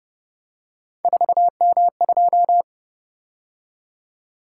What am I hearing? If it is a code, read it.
Morse code, heard as a keyed tone, about 29 wpm, 700 Hz. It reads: 4M2